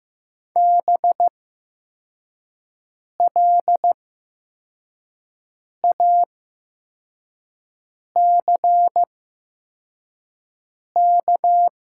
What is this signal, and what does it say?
Morse code, heard as a keyed tone, about 15 wpm, spaced as for 4 wpm, 700 Hz. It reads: BLACK
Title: 8x Character Spacing - Top 500 Words - 15wpm